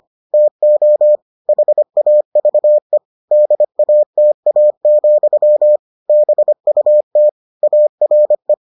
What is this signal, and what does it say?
Morse code, heard as a keyed tone, about 25 wpm, 600 Hz. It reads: TO HAVE DATA, BUT ARE